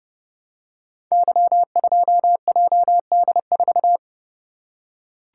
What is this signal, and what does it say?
Morse code, heard as a keyed tone, about 28 wpm, 700 Hz. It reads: Y2JD4
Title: Code Groups - 5 Letter:Numbers - 30wpm